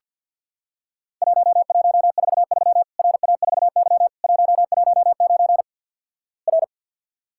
Morse code, heard as a keyed tone, about 50 wpm, 700 Hz. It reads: 1143 RA4Y 119 R
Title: Worked All Europe DX Contest - QTC Exchange - 50wpm